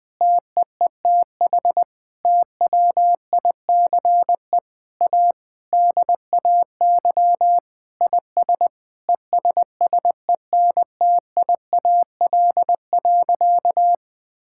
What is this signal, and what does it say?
Morse code, heard as a keyed tone, about 20 wpm, 700 Hz. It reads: TEETH TWICE A DAY IS ESSENTIAL.
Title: Difficult Sentences with Hints - 20wpm